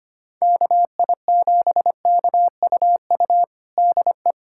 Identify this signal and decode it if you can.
Morse code, heard as a keyed tone, about 25 wpm, 700 Hz. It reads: KI7KUU DE
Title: USA Callsign DE Callsign 25wpm